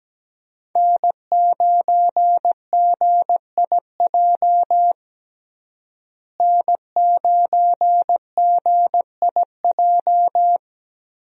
Morse code, heard as a keyed tone, about 17 wpm, 700 Hz. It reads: N9GIJ N9GIJ
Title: USA Call Signs - 5 Characters - Repeated 2x - 17wpm